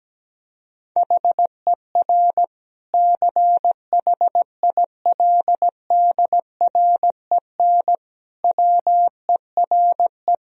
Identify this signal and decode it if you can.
Morse code, heard as a keyed tone, about 17 wpm, 700 Hz. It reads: HER CHILDREN WERE
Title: Sentences from Top 1000 Words - Review Part 4 - 17wpm